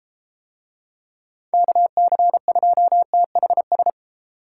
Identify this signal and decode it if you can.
Morse code, heard as a keyed tone, about 32 wpm, 700 Hz. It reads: KC2THS